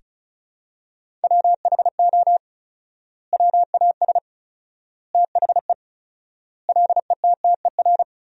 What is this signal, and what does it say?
Morse code, heard as a keyed tone, about 36 wpm, 700 Hz. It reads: WHO WAS THE LETTER
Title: 4x Word-Spacing - Sentences from Top 300 Words - 35wpm